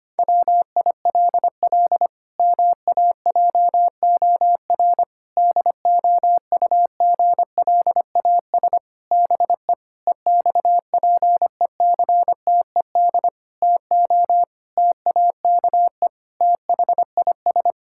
Morse code, heard as a keyed tone, about 25 wpm, 700 Hz. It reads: WILL MAJOR DOUGLAS BE EXPECTED TO TAKE THIS
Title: Pangram Challenge v2 - 25wpm